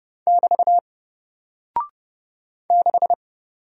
Morse code, heard as a keyed tone, about 30 wpm, 700 Hz. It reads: <BT> 6
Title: Single Character (Letter:Number:Punctuation:Prosign) Alt 30wpm